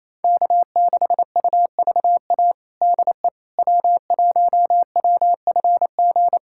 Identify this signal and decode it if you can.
Morse code, heard as a keyed tone, about 28 wpm, 700 Hz. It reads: K6UVA DE W1WFG
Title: USA Callsign DE Callsign - 5 Characters - 28wpm